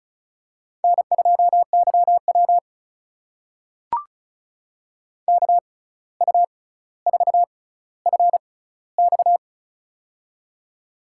Morse code, heard as a keyed tone, about 36 wpm, spaced as for 17 wpm, 700 Hz. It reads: N2YW KU4FX
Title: USA Call Signs - 6x Character Spacing - 35wpm